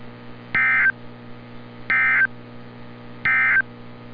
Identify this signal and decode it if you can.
Emergency Alert System (SAME) tones, heard as harsh buzzing bursts. It is heard from 0.4 to 3.7 s.
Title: NOAA Weather Radio - Altoona, PA